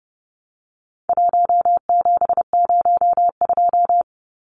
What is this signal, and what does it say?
Morse code, heard as a keyed tone, about 30 wpm, 700 Hz. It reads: 1702
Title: Numbers - 4-Digit 30wpm